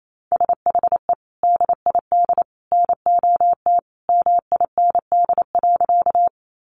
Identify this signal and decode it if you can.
Morse code, heard as a keyed tone, about 28 wpm, 700 Hz. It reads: SHE DID NOT MIND.
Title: Sentences from Top 600 Words 28wpm